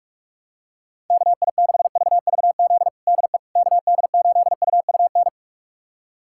Morse code, heard as a keyed tone, about 46 wpm, 700 Hz. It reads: KI6VVZ DE KD8UUN
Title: USA Callsign DE Callsign - 6 Characters - 45wpm